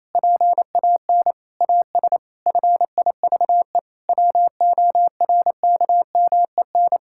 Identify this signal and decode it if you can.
Morse code, heard as a keyed tone, about 28 wpm, 700 Hz. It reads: PAN AS FIVE WORKMEN
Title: Pangram Challenge v2 - 28wpm